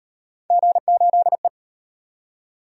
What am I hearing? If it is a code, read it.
Morse code, heard as a keyed tone, about 37 wpm, 700 Hz. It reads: G8E